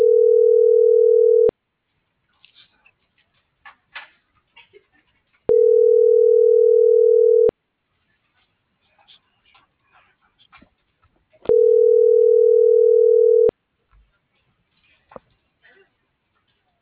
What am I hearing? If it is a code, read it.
Ringback tone (North America), 440 + 480 Hz.